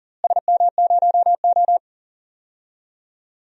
Morse code, heard as a keyed tone, about 40 wpm, 700 Hz. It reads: SM0O
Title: DXCC-100 Call Signs - 4 Characters - Repeated 2x - 40wpm